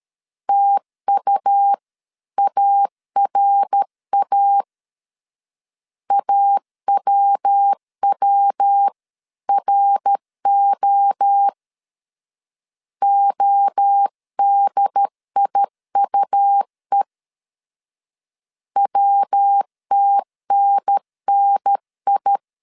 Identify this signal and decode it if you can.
Morse code, heard as a keyed tone, about 13 wpm, 800 Hz. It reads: TUARA AWWRO ODIUE WTNNI
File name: CW / DET / 1 3 1500C S A 12wpm 2905